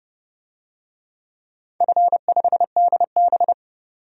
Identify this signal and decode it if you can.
Morse code, heard as a keyed tone, about 30 wpm, 700 Hz. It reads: F5DB
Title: DXCC-100 Call Signs - 4 Characters - 30wpm